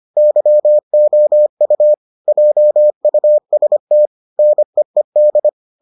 Morse code, heard as a keyed tone, about 25 wpm, 600 Hz. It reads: YOU JUST NEED